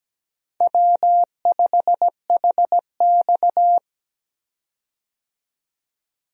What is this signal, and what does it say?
Morse code, heard as a keyed tone, about 17 wpm, 700 Hz. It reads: W5HX
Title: USA Call Signs - 4 Characters - 17wpm